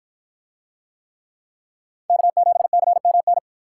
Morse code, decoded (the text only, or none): K7CMN